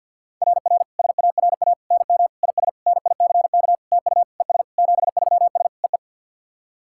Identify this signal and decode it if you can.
Morse code, heard as a keyed tone, about 48 wpm, 690 Hz. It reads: RR SARA NM IS NICK TU ES 73S EE
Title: SOTA QSO Practice 50wpm